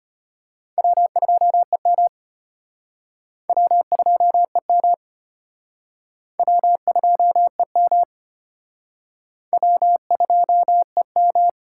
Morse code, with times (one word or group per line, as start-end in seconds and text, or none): 0.8-2.1 W2EM
3.5-4.9 W2EM
6.4-8.0 W2EM
9.5-11.5 W2EM